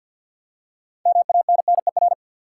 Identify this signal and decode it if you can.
Morse code, heard as a keyed tone, about 47 wpm, 700 Hz. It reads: MANNER
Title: Six Letter Words - 50wpm